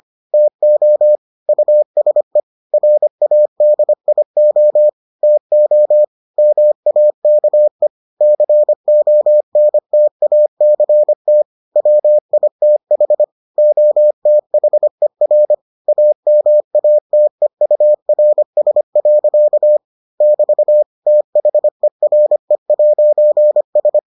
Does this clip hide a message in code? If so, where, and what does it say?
Yes: Morse code from 0.3 to 24.0 s, heard as a keyed tone, about 25 wpm, 600 Hz. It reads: TO USE RADIO TO MAKE CONTACT WITH OTHER AMATEURS. <BT> THERE'S